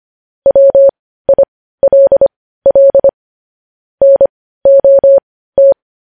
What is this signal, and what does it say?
Morse code, heard as a keyed tone, about 25 wpm, spaced as for 15 wpm, 550 Hz. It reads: WILL NOT
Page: news headlines in morse code at 15 wpm